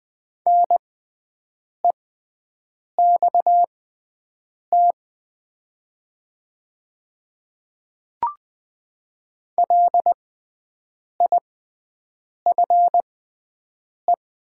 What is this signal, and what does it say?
Morse code, heard as a keyed tone, about 20 wpm, spaced as for 7 wpm, 700 Hz. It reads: NEXT LIFE
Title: 6x Character Spacing - Top 300 Words - 20wpm